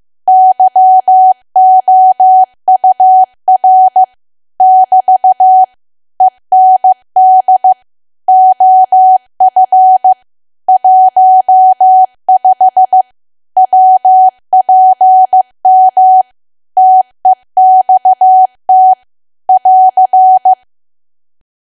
Morse code, heard as a keyed tone, about 15 wpm, 750 Hz. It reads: YOUR <BT> END OF 15 WPM TEXT <AR>
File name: cwa / W1AW-Level2 / 15WPM